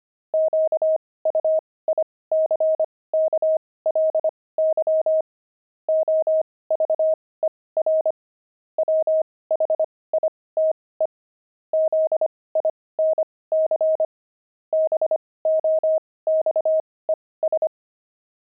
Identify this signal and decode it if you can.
Morse code, heard as a keyed tone, about 25 wpm, spaced as for 18 wpm, 630 Hz. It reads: QUICKLY OVER WHITE ZINC BOXES
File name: pangram-fs-18-cs-25-zinc